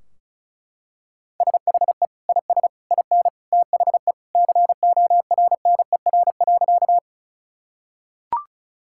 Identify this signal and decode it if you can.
Morse code, heard as a keyed tone, about 36 wpm, 700 Hz. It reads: SHE IS IN THE CORNER.
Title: Sentences from Top 1600 Words 35wpm